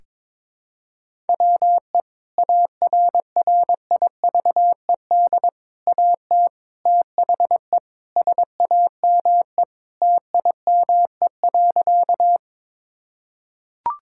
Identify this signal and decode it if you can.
Morse code, heard as a keyed tone, about 22 wpm, 700 Hz. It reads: WE ARRIVED AT THE SAME TIME.
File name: practice / 22wpm / Sentences with 3-Word Phrases - 22wpm